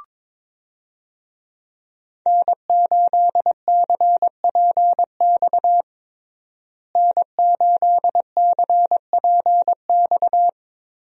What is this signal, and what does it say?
Morse code, heard as a keyed tone, about 22 wpm, 700 Hz. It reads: N8CPX N8CPX
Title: USA Call Signs - 5 Characters - Repeated 3x - 22wpm